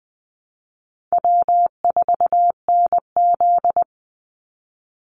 Morse code, heard as a keyed tone, about 20 wpm, 700 Hz. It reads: W4NZ
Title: Call Signs Repeated 3x - 20wpm